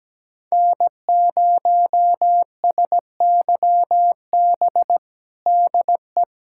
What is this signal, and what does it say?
Morse code, heard as a keyed tone, about 17 wpm, 700 Hz. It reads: N0SYB DE
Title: USA Callsign DE Callsign - 5 Characters - 17wpm